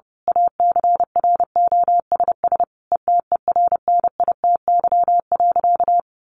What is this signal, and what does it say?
Morse code, heard as a keyed tone, about 30 wpm, 700 Hz. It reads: ACROSS ETERNITY.
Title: A Sound Of Thunder 30wpm